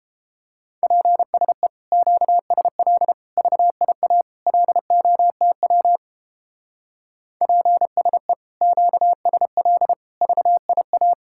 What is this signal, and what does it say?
Morse code, heard as a keyed tone, about 29 wpm, 700 Hz. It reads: PSE QSL VIA LOTW PSE QSL VIA